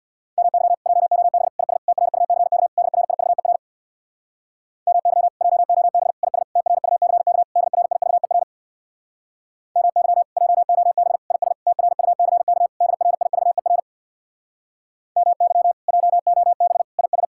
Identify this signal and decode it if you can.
Morse code, heard as a keyed tone, about 64 wpm, 700 Hz. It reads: MY JOB IS TRUCK DRIVER MY JOB IS TRUCK DRIVER MY JOB IS TRUCK DRIVER MY JOB IS